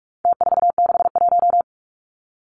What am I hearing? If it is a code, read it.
Morse code, heard as a keyed tone, about 44 wpm, 700 Hz. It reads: T461
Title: Numbers - Contest Style - 4-Digit 45wpm